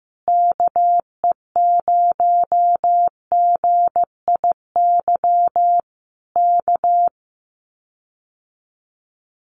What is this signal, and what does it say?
Morse code, heard as a keyed tone, about 15 wpm, 700 Hz. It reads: KE0GIY K
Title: Common QSO Phrases Encore 15wpm